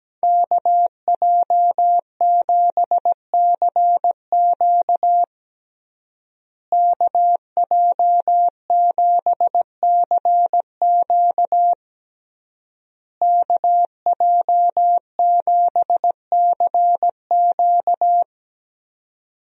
Morse code, heard as a keyed tone, about 17 wpm, 700 Hz. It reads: KJ7CQ KJ7CQ KJ7CQ
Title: USA Call Signs - 5 Characters - Repeated 3x - 17wpm